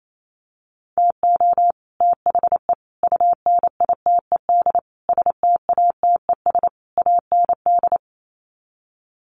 Morse code, heard as a keyed tone, about 28 wpm, 700 Hz. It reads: TO THE UNITED STATES AND
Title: Sets of 5 Words Encore 28wpm